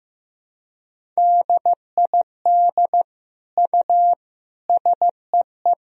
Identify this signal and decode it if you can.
Morse code, heard as a keyed tone, about 16 wpm, 700 Hz. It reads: DID U SEE
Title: Sentences from Top 100 Words and Abbreviations 15wpm